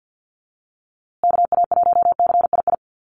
Morse code, heard as a keyed tone, about 50 wpm, 700 Hz. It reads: KA1CII
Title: USA Call Signs Repeated 3x 50wpm